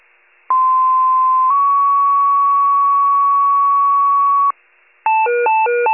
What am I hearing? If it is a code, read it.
Two-tone page, heard as a one-second tone, then a three-second tone, about 1040 then 1120 Hz. It is heard from 0.5 to 4.5 s.
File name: pager / 95 RESCUE / wcalert